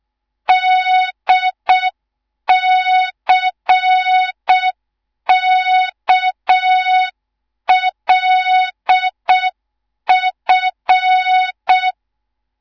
Morse code, heard as a keyed tone, about 6 wpm, 3700 Hz. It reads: DCKLF
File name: Morze-HA2MN / hang / 18 IRASJELEK2